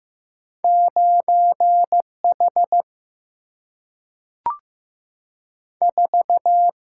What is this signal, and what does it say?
Morse code, heard as a keyed tone, about 15 wpm, 700 Hz. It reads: 9 H 4
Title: Single-Letter Single-Number 15wpm